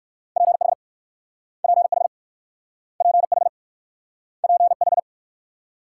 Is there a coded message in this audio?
Yes: Morse code from 0.4 to 5.0 s, heard as a keyed tone, about 61 wpm, 700 Hz. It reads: PH PH PH PH